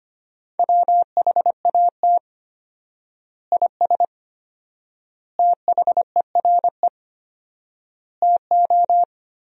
Morse code, heard as a keyed tone, about 25 wpm, 700 Hz. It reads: WHAT IS THERE TO
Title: 4x Word-Spacing - Sentences from Top 400 Words - 25wpm